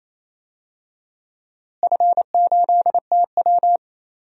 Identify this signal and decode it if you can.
Morse code, heard as a keyed tone, about 28 wpm, 700 Hz. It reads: F8TW
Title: DXCC-100 Call Signs - 4 Characters - Repeated 2x - 28wpm